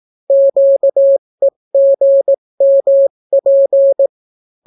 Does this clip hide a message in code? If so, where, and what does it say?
Yes: Morse code from 0.3 to 4.1 s, heard as a keyed tone, about 18 wpm, 550 Hz. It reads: QEGMP